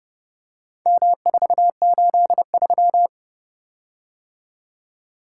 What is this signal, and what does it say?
Morse code, heard as a keyed tone, about 29 wpm, 700 Hz. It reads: M483